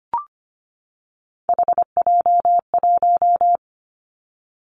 Morse code, heard as a keyed tone, about 25 wpm, 700 Hz. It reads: HJ1
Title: Three Challenging Letter-Number Combinations 25wpm